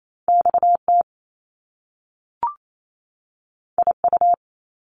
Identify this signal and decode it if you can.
Morse code, heard as a keyed tone, about 28 wpm, 700 Hz. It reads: X T I U
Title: Top Two Letter Combinations in Words 28wpm